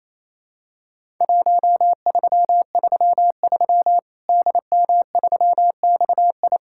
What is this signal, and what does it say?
Morse code, heard as a keyed tone, about 28 wpm, 700 Hz. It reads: 1333 DM3XI
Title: Worked All Europe DX Contest - QTC Exchange - 28wpm